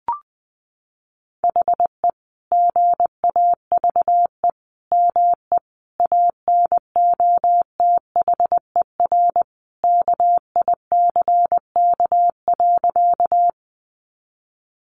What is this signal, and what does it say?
Morse code, heard as a keyed tone, about 20 wpm, 700 Hz. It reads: HE GAVE ME ANOTHER KICK.
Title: Sentences from Top 1300 Words 20wpm